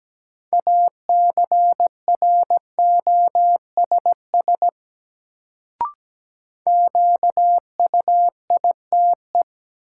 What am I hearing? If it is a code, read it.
Morse code, heard as a keyed tone, about 17 wpm, 700 Hz. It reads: ACROSS QUITE
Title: Sentences from Top 700 Words - New Word Drill - 17wpm